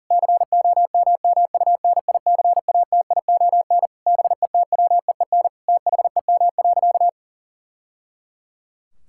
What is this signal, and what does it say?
Morse code, heard as a keyed tone, about 40 wpm, 700 Hz. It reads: COMMUNICATION BETWEEN THEM.
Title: Sentences from Top 1000 Words - Review Part 6 - 40wpm